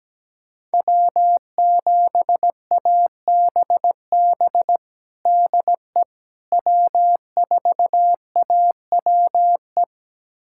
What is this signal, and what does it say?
Morse code, heard as a keyed tone, about 17 wpm, 700 Hz. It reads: W7ABB DE W4AWE
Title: USA Callsign DE Callsign - 5 Characters - 17wpm